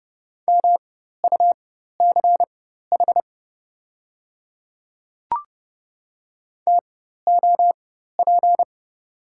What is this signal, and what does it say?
Morse code, heard as a keyed tone, about 29 wpm, spaced as for 14 wpm, 700 Hz. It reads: MUCH TOP
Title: 4x Character Spacing - Top 500 Words - 30wpm